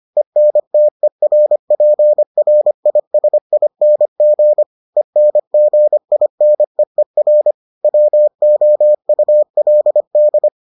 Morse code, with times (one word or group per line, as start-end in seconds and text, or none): 0.2-4.6 ENTERPRISING
5.0-7.5 ENGINEER
7.8-10.5 WOULD